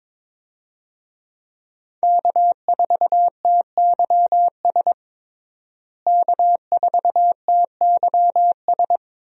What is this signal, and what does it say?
Morse code, heard as a keyed tone, about 22 wpm, 700 Hz. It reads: K4TYS K4TYS
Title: USA Call Signs - 5 Characters - Repeated 2x - 22wpm